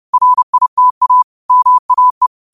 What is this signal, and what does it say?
Morse code, heard as a keyed tone, about 30 wpm, 1000 Hz. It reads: RITA MAE